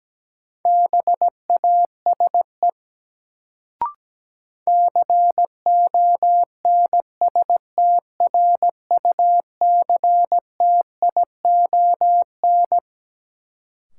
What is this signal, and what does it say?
Morse code, heard as a keyed tone, about 17 wpm, 700 Hz. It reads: BASE CONSTRUCTION
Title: Sentences from Top 1600 Words - New Word Drill Alt - 17wpm